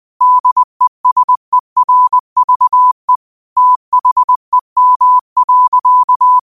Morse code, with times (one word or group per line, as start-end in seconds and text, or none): 0.2-3.2 DESERVE
3.6-6.4 THEM.